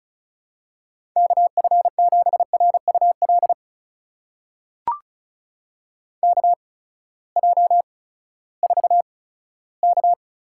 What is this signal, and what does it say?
Morse code, heard as a keyed tone, about 36 wpm, spaced as for 18 wpm, 700 Hz. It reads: KF7RUL KJ4K
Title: USA Call Signs - 6 Characters - 8x Character Spacing - 35wpm